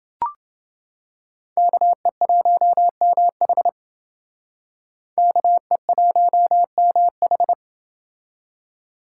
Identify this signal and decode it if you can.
Morse code, heard as a keyed tone, about 28 wpm, 700 Hz. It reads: KE1MH KE1MH